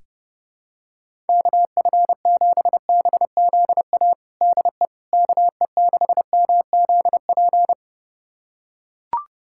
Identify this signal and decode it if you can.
Morse code, heard as a keyed tone, about 30 wpm, 700 Hz. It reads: KF7BZA DE KE6MZP
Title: USA Callsign DE Callsign - 6 Characters - 30wpm